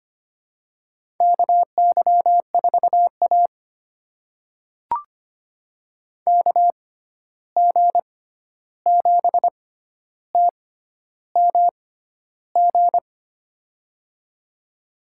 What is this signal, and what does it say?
Morse code, heard as a keyed tone, about 25 wpm, spaced as for 11 wpm, 700 Hz. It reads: KY4A KG7TMG
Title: USA Call Signs - 6x Character Spacing - 25wpm